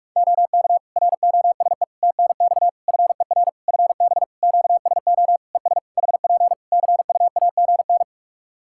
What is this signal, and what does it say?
Morse code, heard as a keyed tone, about 46 wpm, 700 Hz. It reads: OK ROSE TNX FER FB QSO ES HP CUAGN
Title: Practice QSO - Exchange 4 - Ending - 45wpm